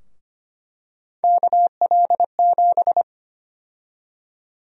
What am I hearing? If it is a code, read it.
Morse code, heard as a keyed tone, about 25 wpm, 700 Hz. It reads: KL7
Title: USA Call Sign Prefixes - Repeated 2x - 25wpm